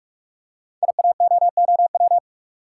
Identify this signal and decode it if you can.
Morse code, heard as a keyed tone, about 44 wpm, 690 Hz. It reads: IAOOW
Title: Code Groups - 5 Letters - 45wpm